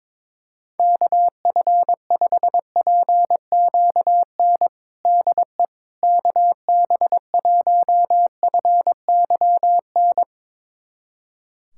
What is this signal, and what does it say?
Morse code, heard as a keyed tone, about 22 wpm, 700 Hz. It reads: KF5PQN DE KB1FYN
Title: USA Callsign DE Callsign - 6 Characters - 22wpm